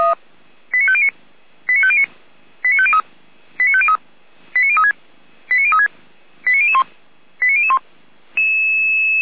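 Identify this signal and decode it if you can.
Five-tone selective call (ZVEI), heard as quick runs of five tones, tones of about 70 ms. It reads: 89509 89509 89503 89503 89036 89036 89001 89001
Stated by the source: ZVEI five-tone call received by SDR, on the audio server of Feuerwehr Orlamünde (fire brigade).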